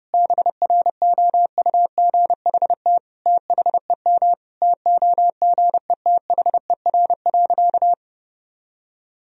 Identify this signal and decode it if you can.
Morse code, heard as a keyed tone, about 31 wpm, 700 Hz. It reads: BROUGHT THEM TOGETHER.
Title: Sentences from Top 1000 Words - Review Part 3 - 30wpm